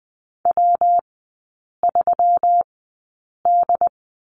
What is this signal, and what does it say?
Morse code, heard as a keyed tone, about 20 wpm, spaced as for 8 wpm, 690 Hz. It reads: W3D
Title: 159 Lesson 26e - Letter g - Three Characters 20wpm